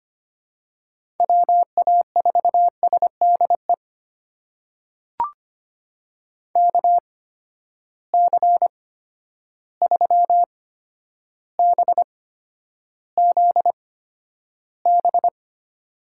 Morse code, heard as a keyed tone, about 25 wpm, spaced as for 11 wpm, 700 Hz. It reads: WA4SDE KC3BZB
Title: USA Call Signs - 6 Characters - 8x Character Spacing - 25wpm